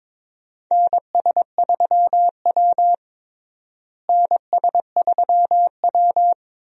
Morse code, heard as a keyed tone, about 22 wpm, 700 Hz. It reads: NS3W NS3W
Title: USA Call Signs - 4 Characters - Repeated 3x - 22wpm